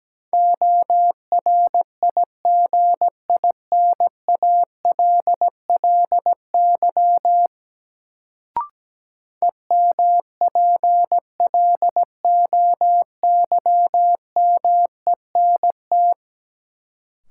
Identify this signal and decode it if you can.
Morse code, heard as a keyed tone, about 17 wpm, 700 Hz. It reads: ORIGINALLY EMPLOYMENT
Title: Sentences from Top 1500 Words - New Word Drill Alt - 17wpm